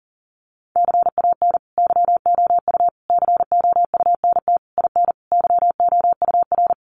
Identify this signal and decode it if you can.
Morse code, heard as a keyed tone, about 40 wpm, 700 Hz. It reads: CAN YOU COUNT IN YOUR
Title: Sentences from Top 1500 Words 40wpm